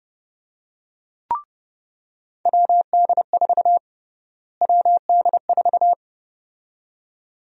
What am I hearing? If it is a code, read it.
Morse code, heard as a keyed tone, about 29 wpm, 700 Hz. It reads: WD4 WD4
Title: USA Call Sign Prefixes - Repeated 2x - 30wpm